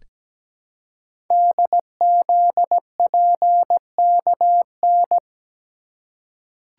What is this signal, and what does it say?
Morse code, heard as a keyed tone, about 17 wpm, 700 Hz. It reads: DZPKN
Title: Code Groups - 5 Letters - 17wpm